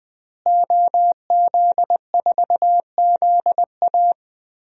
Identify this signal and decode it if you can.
Morse code, heard as a keyed tone, about 20 wpm, 700 Hz. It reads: OZ4ZA
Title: DXCC-100 Call Signs - 5 Characters - Repeated 2x - 20wpm